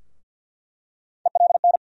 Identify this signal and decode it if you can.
Morse code, heard as a keyed tone, about 47 wpm, 700 Hz. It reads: ELN
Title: USA Call Sign Postfixes - 3 Letters 50wpm